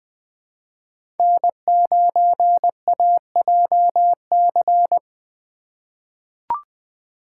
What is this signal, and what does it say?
Morse code, heard as a keyed tone, about 20 wpm, 700 Hz. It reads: N9AJC
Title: USA Call Signs - 3x Character Spacing - 20wpm